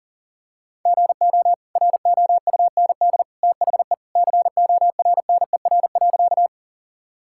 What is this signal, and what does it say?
Morse code, heard as a keyed tone, about 40 wpm, 700 Hz. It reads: GO ROUND THE CORNER.
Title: Sentences from Top 1600 Words 40wpm